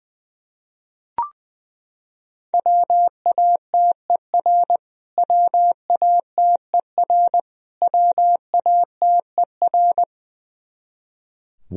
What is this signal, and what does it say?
Morse code, heard as a keyed tone, about 20 wpm, 700 Hz. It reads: WATER WATER WATER
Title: Top 100 Words Repeated 20wpm